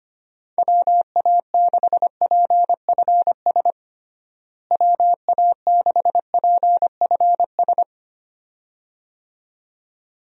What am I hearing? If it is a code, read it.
Morse code, heard as a keyed tone, about 25 wpm, 700 Hz. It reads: WA6PFS WA6PFS
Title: USA Call Signs - 6 Characters - Repeated 2x - 25wpm